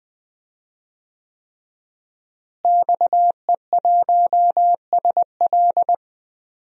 Morse code, heard as a keyed tone, about 20 wpm, 700 Hz. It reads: XE1SL
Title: DXCC-100 Call Signs - 5 Characters - Repeated 3x - 20wpm